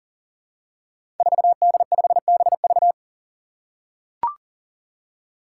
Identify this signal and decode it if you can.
Morse code, heard as a keyed tone, about 40 wpm, 700 Hz. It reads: 4D5BV
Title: Five Challenging Letter-Number Combinations 40wpm